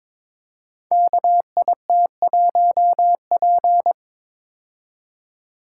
Morse code, heard as a keyed tone, about 22 wpm, 700 Hz. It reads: KIT1P
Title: Code Groups - 5 Letter:Numbers - 22wpm